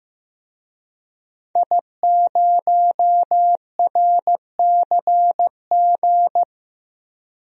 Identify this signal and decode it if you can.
Morse code, heard as a keyed tone, about 15 wpm, 700 Hz. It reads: I0RCG
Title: DXCC-100 Call Signs - 5 Characters - Repeated 2x - 15wpm